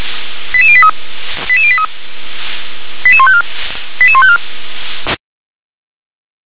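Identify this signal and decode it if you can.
Five-tone selective call (ZVEI), heard as quick runs of five tones, tones of about 70 ms. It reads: 88083 88083 88154 88154
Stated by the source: ZVEI five-tone selective call 88641 received by SDR.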